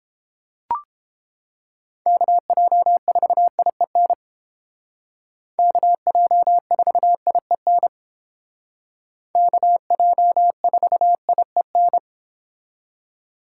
Morse code, heard as a keyed tone, about 29 wpm, 700 Hz. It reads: KJ4IEN KJ4IEN KJ4IEN